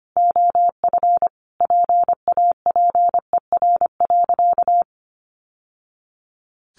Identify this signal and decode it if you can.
Morse code, heard as a keyed tone, about 25 wpm, 700 Hz. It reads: OF PAPER.
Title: Sentences from Top 900 Words 25wpm